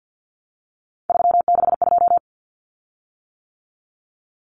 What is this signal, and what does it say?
Morse code, heard as a keyed tone, about 50 wpm, 700 Hz. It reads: 362
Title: Common Cut Numbers - 3-Digit 50wpm